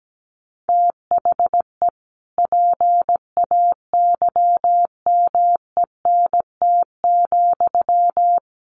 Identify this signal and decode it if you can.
Morse code, heard as a keyed tone, about 17 wpm, 700 Hz. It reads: THE PAYMENT,
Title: Sentences from Top 1200 Words 17wpm